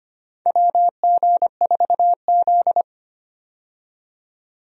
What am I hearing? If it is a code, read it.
Morse code, heard as a keyed tone, about 25 wpm, 700 Hz. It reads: WG4Z
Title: USA Call Signs - 4 Characters - 25wpm